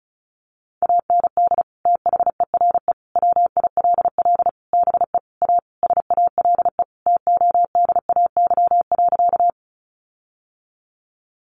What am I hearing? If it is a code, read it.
Morse code, heard as a keyed tone, about 34 wpm, 700 Hz. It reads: AND THERE WILL BE A SALE TODAY.
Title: Sentences from Top 1400 Words 35wpm